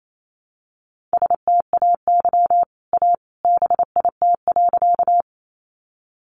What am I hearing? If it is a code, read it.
Morse code, heard as a keyed tone, about 28 wpm, 700 Hz. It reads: STAY A BIT.
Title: Sentences from Top 900 Words 28wpm